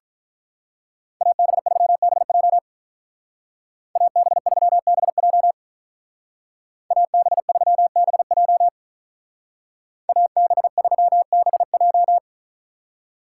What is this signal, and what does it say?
Morse code, heard as a keyed tone, about 43 wpm, 700 Hz. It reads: AB3BJ AB3BJ AB3BJ AB3BJ
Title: Speed Racing - USA 2x2 Call Signs 35wpm